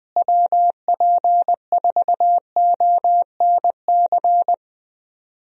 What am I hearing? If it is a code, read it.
Morse code, heard as a keyed tone, about 20 wpm, 700 Hz. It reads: WP4ONC
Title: USA Call Signs - 6 Characters - Repeated 3x - 20wpm